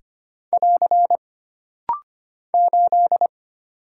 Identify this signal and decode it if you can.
Morse code, heard as a keyed tone, about 25 wpm, 700 Hz. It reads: <AR> 8
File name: Single Character (Letter:Number:Punctuation:Prosign) - Rapid-Fire - 25wpm